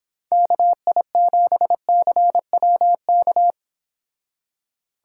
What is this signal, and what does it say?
Morse code, heard as a keyed tone, about 26 wpm, 700 Hz. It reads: KI7CWK